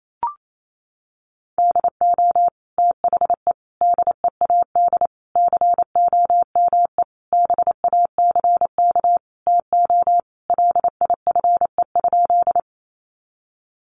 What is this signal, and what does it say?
Morse code, heard as a keyed tone, about 28 wpm, 700 Hz. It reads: DO THE DEAD COME BACK TO LIFE?
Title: Sentences from Top 1500 Words 28wpm